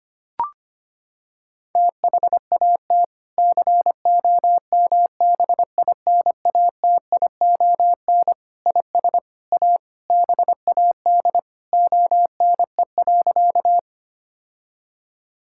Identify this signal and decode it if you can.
Morse code, heard as a keyed tone, about 25 wpm, 700 Hz. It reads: THAT COMBINATION IS A BAD ONE.
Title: Sentences from Top 1000 Words - Review Part 6 - 25wpm